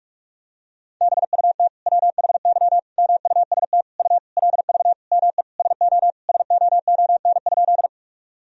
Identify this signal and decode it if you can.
Morse code, heard as a keyed tone, about 46 wpm, 700 Hz. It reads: BUT WHY MUST U LV ME SO SOON?
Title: Sentences from Top 400 Words and Abbreviations 45wpm